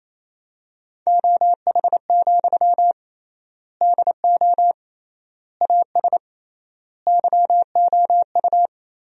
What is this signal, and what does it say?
Morse code, heard as a keyed tone, about 28 wpm, 700 Hz. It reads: OH, DO AS YOU
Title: 3x Word-Spacing - Sentences from Top 500 Words - 28wpm